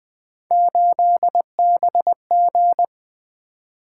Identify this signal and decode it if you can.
Morse code, heard as a keyed tone, about 20 wpm, 700 Hz. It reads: 8BG
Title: Three Letter-Number Combinations - 20wpm